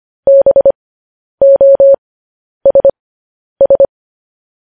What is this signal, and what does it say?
Morse code, heard as a keyed tone, about 25 wpm, spaced as for 10 wpm, 550 Hz. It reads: BOSS
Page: news headlines in morse code at 10 wpm